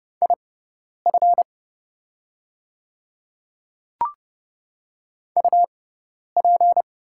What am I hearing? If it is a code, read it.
Morse code, heard as a keyed tone, about 31 wpm, spaced as for 10 wpm, 700 Hz. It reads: IF UP
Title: 6x Character Spacing - Top 100 Words - 30wpm